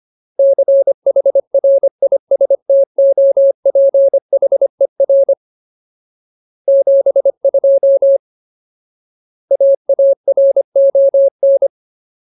Morse code, heard as a keyed tone, about 25 wpm, 550 Hz. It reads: CHRISTOPHER 72 AARON
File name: Ultimate Word List / 25WPM Ultimate